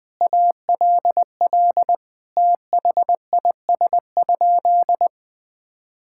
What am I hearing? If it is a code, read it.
Morse code, heard as a keyed tone, about 20 wpm, 700 Hz. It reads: ALL THIS?
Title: Sentences from Top 1000 Words - Review Part 6 - 20wpm